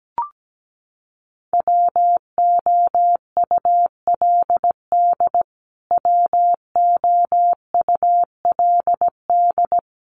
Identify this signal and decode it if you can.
Morse code, heard as a keyed tone, about 17 wpm, 700 Hz. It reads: WOULD WOULD
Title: Top 200 Words Repeated 17wpm